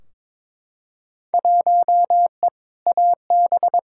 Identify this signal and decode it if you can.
Morse code, heard as a keyed tone, about 22 wpm, 700 Hz. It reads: 1E AB
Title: Field Day Exchanges - 22wpm